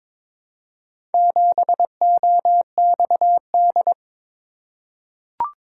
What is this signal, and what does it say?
Morse code, heard as a keyed tone, about 22 wpm, 700 Hz. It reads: 7OXD
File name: practice / 22wpm / Four Letter-Number Combinations - 22wpm